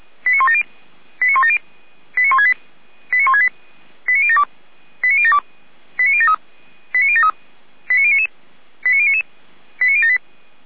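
Five-tone selective call (ZVEI), heard as quick runs of five tones, tones of about 70 ms. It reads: 89170 89170 89168 89168 89072 89072 89073 89073 89099 89099 89078
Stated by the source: ZVEI five-tone selective call received on an SDR.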